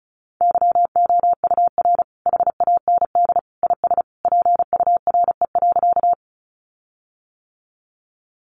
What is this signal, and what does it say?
Morse code, heard as a keyed tone, about 34 wpm, 700 Hz. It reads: YOUR HAND IS PURE.
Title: Sentences from Top 1500 Words 35wpm